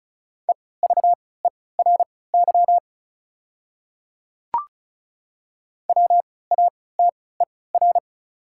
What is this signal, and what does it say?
Morse code, heard as a keyed tone, about 36 wpm, spaced as for 20 wpm, 700 Hz. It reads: EVERY WATER
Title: 3x Character Spacing - Top 200 Words - 35wpm